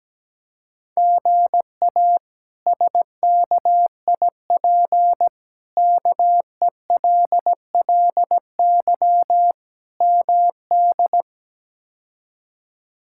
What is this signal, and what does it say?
Morse code, heard as a keyed tone, about 17 wpm, 700 Hz. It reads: GA SKIP KELLY MD
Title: K1USN SST QSO Practice - 17wpm